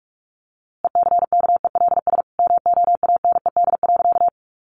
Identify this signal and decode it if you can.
Morse code, heard as a keyed tone, about 45 wpm, 700 Hz. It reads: ECKELS MOANED.
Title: A Sound Of Thunder 45wpm